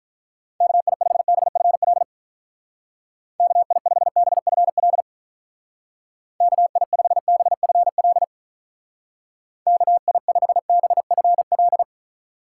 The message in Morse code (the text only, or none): KI5BFL KI5BFL KI5BFL KI5BFL